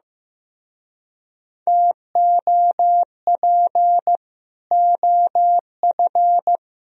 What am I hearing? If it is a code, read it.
Morse code, heard as a keyed tone, about 18 wpm, 700 Hz. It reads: TOP OF
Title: Speed-Racing - Sets of 3 Words 15wpm